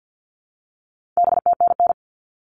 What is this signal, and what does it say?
Morse code, heard as a keyed tone, about 50 wpm, 700 Hz. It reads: BTNN
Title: All Cut Numbers - 4-Digit 50wpm